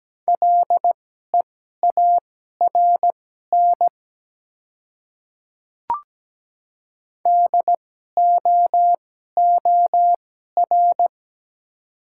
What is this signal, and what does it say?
Morse code, heard as a keyed tone, about 17 wpm, spaced as for 12 wpm, 700 Hz. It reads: LEARN DOOR
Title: 2x Character Spacing - Top 300 Words - 17wpm